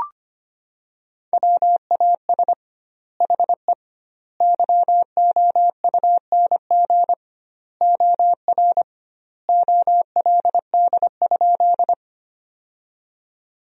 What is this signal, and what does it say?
Morse code, heard as a keyed tone, about 25 wpm, 700 Hz. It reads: WAS HE YOUNG OR OLD?
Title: 2x Word-Spacing - Sentences from Top 400 Words - 25wpm